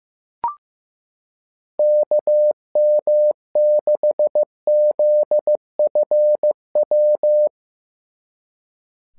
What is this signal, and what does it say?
Morse code, heard as a keyed tone, about 15 wpm, 600 Hz. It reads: KM6ZFW
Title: SOTA QSO Practice 15wpm